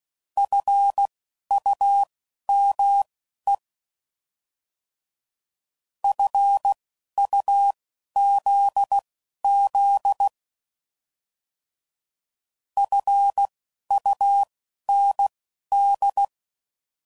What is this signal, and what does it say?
Morse code, heard as a keyed tone, about 16 wpm, spaced as for 11 wpm, 800 Hz. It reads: FUME FUZZ FUND
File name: Prefixes / By First Letter / Fs